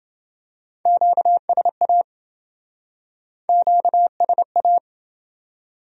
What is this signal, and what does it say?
Morse code, heard as a keyed tone, about 27 wpm, 700 Hz. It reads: QSA QSA